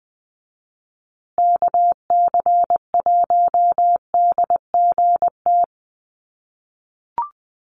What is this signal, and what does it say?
Morse code, heard as a keyed tone, about 20 wpm, 700 Hz. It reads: KC1DGT
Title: USA Call Signs Repeated 3x 20wpm